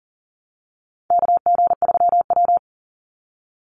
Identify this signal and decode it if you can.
Morse code, heard as a keyed tone, about 40 wpm, 700 Hz. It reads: KG3W
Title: NA SOTA Chaser Callsigns 40wpm